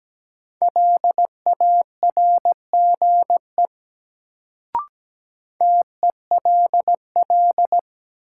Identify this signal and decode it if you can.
Morse code, heard as a keyed tone, about 17 wpm, 700 Hz. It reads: LARGE TELL
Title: Sentences from Top 200 Words - New Word Drill - 17wpm